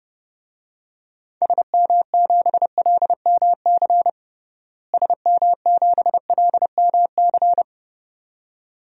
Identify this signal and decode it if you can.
Morse code, heard as a keyed tone, about 30 wpm, 700 Hz. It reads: SM7LMC SM7LMC
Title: DXCC-100 Call Signs - 6 Characters - Repeated 2x - 30wpm